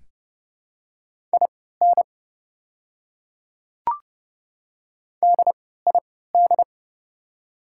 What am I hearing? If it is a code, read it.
Morse code, heard as a keyed tone, about 30 wpm, spaced as for 17 wpm, 700 Hz. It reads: IN DID
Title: 3x Character Spacing - Top 100 Words - 30wpm